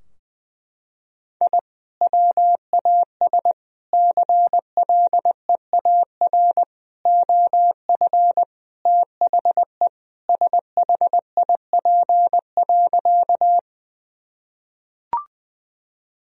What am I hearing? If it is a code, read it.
Morse code, heard as a keyed tone, about 20 wpm, 700 Hz. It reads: I WAS CLEAR OF THE SHIP.
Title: Sentences from Top 1000 Words - Review Part 2 - 20wpm